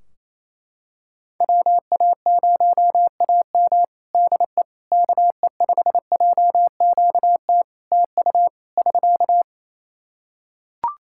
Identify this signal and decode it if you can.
Morse code, heard as a keyed tone, about 28 wpm, 700 Hz. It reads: WA0AM DE KE5JQT TU <SK>
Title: Practice QSO - Exchange 4 - Ending - 28wpm